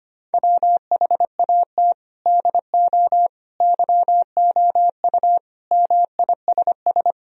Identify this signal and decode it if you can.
Morse code, heard as a keyed tone, about 25 wpm, 700 Hz. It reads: WHAT DO YOU MISS
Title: Sentences from Top 1000 Words - Review Part 3 - 25wpm